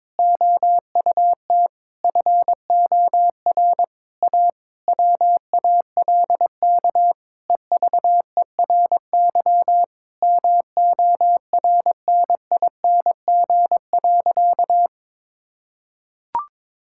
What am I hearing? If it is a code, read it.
Morse code, heard as a keyed tone, about 22 wpm, 700 Hz. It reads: OUT FOR A WALK EVERY MORNING.
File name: Difficult Sentences with Hints - 22wpm